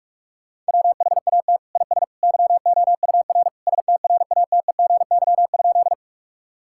Morse code, heard as a keyed tone, about 46 wpm, 700 Hz. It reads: WHAT IS YOUR STRATEGY?
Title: Sentences from Top 1000 Words - Review Part 5 - 45wpm